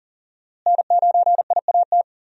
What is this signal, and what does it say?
Morse code, heard as a keyed tone, about 40 wpm, 700 Hz. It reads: N9IAT
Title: USA Call Signs - 5 Characters - Repeated 2x - 40wpm